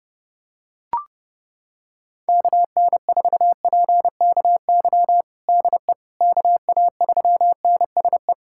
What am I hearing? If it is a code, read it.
Morse code, heard as a keyed tone, about 30 wpm, 700 Hz. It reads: KN4PKY DE KA3NSE
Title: USA Callsign DE Callsign - 6 Characters - 30wpm